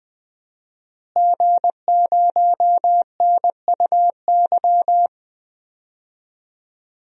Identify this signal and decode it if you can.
Morse code, heard as a keyed tone, about 20 wpm, 700 Hz. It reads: G0NUY
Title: DXCC-100 Call Signs - 5 Characters - Repeated 2x - 20wpm